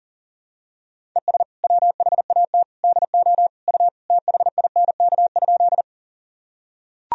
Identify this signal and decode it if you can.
Morse code, heard as a keyed tone, about 40 wpm, 700 Hz. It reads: ES WHAT DO U THINK?
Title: Sentences from Top 200 Words and Abbreviations 40wpm